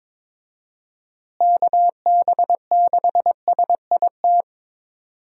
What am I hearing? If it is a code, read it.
Morse code, heard as a keyed tone, about 22 wpm, 700 Hz. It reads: KB6SIT
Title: USA Call Signs - 6 Characters - Repeated 3x - 22wpm